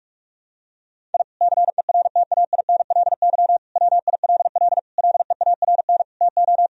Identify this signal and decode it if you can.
Morse code, heard as a keyed tone, about 45 wpm, 700 Hz. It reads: I CERTAINLY WILL LEARN TO
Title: Sentences from Top 1000 Words - Review Part 4 - 45wpm